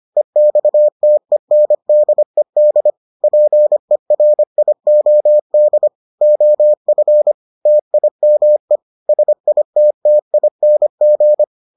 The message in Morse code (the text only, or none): EXTENDED PERIOD OF TIME SITTING